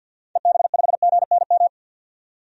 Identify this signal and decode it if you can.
Morse code, heard as a keyed tone, about 47 wpm, 700 Hz. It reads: EB5GNM
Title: DXCC-100 Call Signs - 6 Characters - 50wpm